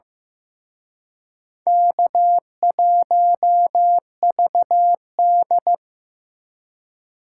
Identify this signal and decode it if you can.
Morse code, heard as a keyed tone, about 18 wpm, 700 Hz. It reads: K1VD
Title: Speed Racing - USA 1x2 Call Signs 15wpm